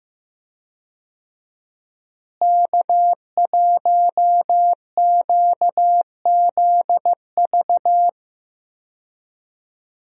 Morse code, heard as a keyed tone, about 15 wpm, 700 Hz. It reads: K1QZV
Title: USA Call Signs - 5 Characters - Repeated 2x - 15wpm